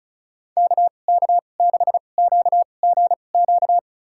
Morse code, heard as a keyed tone, about 36 wpm, spaced as for 25 wpm, 700 Hz. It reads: KK6QGQ
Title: USA Call Signs - 6 Characters - 2x Character Spacing - 35wpm